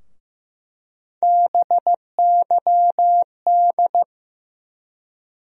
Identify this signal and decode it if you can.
Morse code, heard as a keyed tone, about 15 wpm, 700 Hz. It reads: BYD
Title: USA Call Sign Postfixes - 3 Letters 15wpm